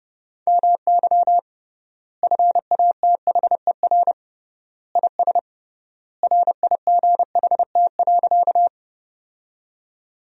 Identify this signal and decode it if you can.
Morse code, heard as a keyed tone, about 32 wpm, 700 Hz. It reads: MY FATHER IS RIGHT.
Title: 3x Word-Spacing - Sentences from Top 300 Words - 30wpm